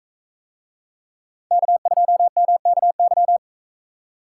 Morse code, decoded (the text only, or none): K2MKY